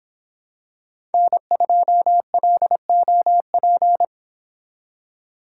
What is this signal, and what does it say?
Morse code, heard as a keyed tone, about 25 wpm, 700 Hz. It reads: N2LOP